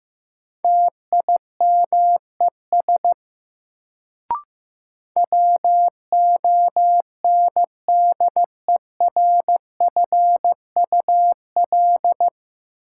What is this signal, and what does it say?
Morse code, heard as a keyed tone, about 15 wpm, 700 Hz. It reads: TIMES WONDERFUL
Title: Sentences from Top 1100 Words - New Word Drill Alt - 15wpm